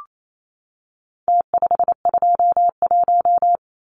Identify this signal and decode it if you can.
Morse code, heard as a keyed tone, about 28 wpm, 700 Hz. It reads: T521
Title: Numbers - Contest Style - 4-Digit 28wpm